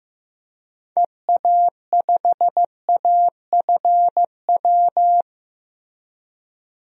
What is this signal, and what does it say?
Morse code, heard as a keyed tone, about 15 wpm, 700 Hz. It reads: EA5AFW
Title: DXCC-100 Call Signs - 6 Characters - Repeated 2x - 15wpm